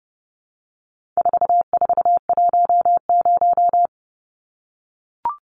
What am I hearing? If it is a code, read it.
Morse code, heard as a keyed tone, about 30 wpm, 700 Hz. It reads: 4410
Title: Numbers - 4-Digit 30wpm